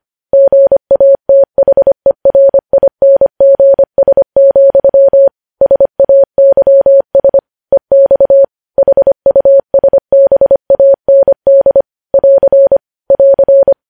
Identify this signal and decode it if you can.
Morse code, heard as a keyed tone, about 25 wpm, 550 Hz. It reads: GATHERINGS, SAYS EX HUSBAND <AR> <AR>
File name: podcast / News headlines-1x-25x25